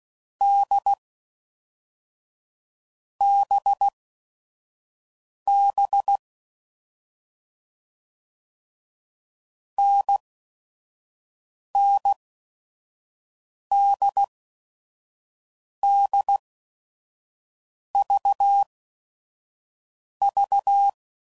Morse code, heard as a keyed tone, about 16 wpm, spaced as for 5 wpm, 800 Hz. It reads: D BB NNDDVV
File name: Combinations / Difficult Combos / NDB NDB6